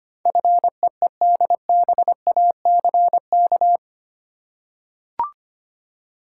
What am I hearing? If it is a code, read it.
Morse code, heard as a keyed tone, about 25 wpm, 700 Hz. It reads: FEEDBACK
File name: Eight Letter Words - 25wpm